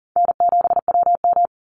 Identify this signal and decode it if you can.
Morse code, heard as a keyed tone, about 40 wpm, 700 Hz. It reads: N7WM
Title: NA SOTA Chaser Callsigns 40wpm